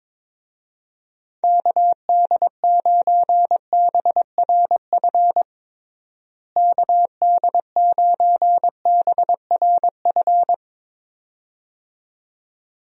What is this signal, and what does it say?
Morse code, heard as a keyed tone, about 22 wpm, 700 Hz. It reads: KD9BRF KD9BRF
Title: USA Call Signs - 6 Characters - Repeated 2x - 22wpm